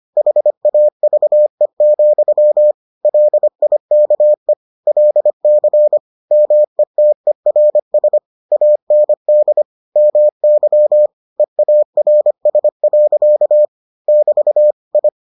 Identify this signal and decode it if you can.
Morse code, heard as a keyed tone, about 25 wpm, 600 Hz. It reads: HAVE, LIKE LC METERS AND MY EARS. <BT> I